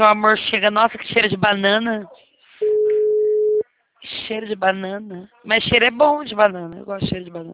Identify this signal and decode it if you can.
Ringback tone (Europe, 425 Hz), with speech.
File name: out- -2022